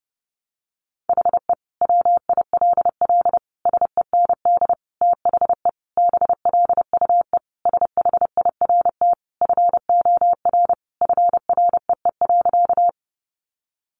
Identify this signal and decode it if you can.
Morse code, heard as a keyed tone, about 30 wpm, 700 Hz. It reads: HE WILL SEND THE BLUE SHIRT FOR FREE.
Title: Additional Easy Phrases 30wpm